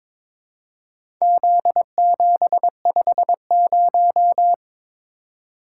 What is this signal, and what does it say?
Morse code, heard as a keyed tone, about 21 wpm, 700 Hz. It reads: Z750
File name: practice / 22wpm / Four Letter-Number Combinations - 22wpm